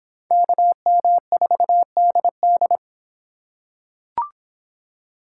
Morse code, heard as a keyed tone, about 25 wpm, 700 Hz. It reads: KM4DD